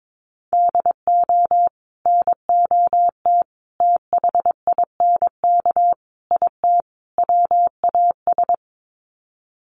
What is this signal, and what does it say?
Morse code, heard as a keyed tone, about 22 wpm, 700 Hz. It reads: DO NOT THINK IT WAS
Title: Sets of 5 Words Encore 22wpm